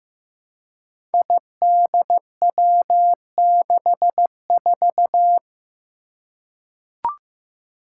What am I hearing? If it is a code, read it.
Morse code, heard as a keyed tone, about 15 wpm, 700 Hz. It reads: IDW64
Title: Five Challenging Letter-Number Combinations 15wpm